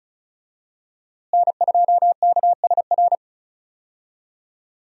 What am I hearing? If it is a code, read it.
Morse code, heard as a keyed tone, about 35 wpm, 700 Hz. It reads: N2KSR
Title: USA Call Signs - 5 Characters - Repeated 3x - 35wpm